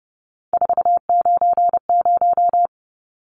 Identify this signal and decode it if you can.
Morse code, heard as a keyed tone, about 30 wpm, 700 Hz. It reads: 490
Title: Numbers - 3-Digit 30wpm